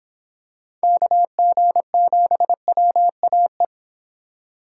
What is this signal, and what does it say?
Morse code, heard as a keyed tone, about 26 wpm, 700 Hz. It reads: KG7WAE